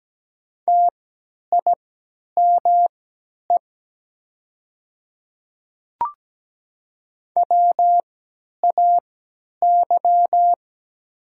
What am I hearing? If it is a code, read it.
Morse code, heard as a keyed tone, about 17 wpm, spaced as for 10 wpm, 700 Hz. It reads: TIME WAY
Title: 3x Character Spacing - Top 100 Words - 17wpm